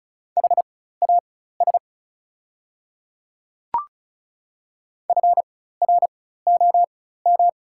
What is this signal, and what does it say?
Morse code, heard as a keyed tone, about 34 wpm, spaced as for 16 wpm, 700 Hz. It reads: HAS FROM
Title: 4x Character Spacing - Top 100 Words - 35wpm